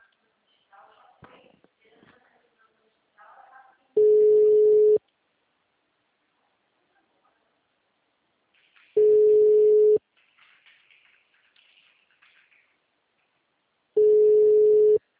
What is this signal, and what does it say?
Ringback tone (Europe, 425 Hz), with speech.